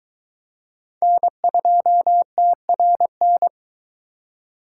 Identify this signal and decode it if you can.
Morse code, heard as a keyed tone, about 22 wpm, 700 Hz. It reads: N2TRN